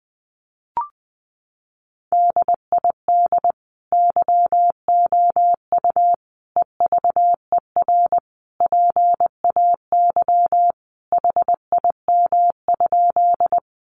Morse code, heard as a keyed tone, about 20 wpm, 700 Hz. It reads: DID YOU EVER PAY HIM?
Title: Sentences from Top 700 Words 20wpm